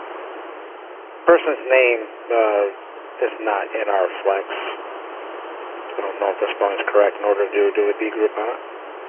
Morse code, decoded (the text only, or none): none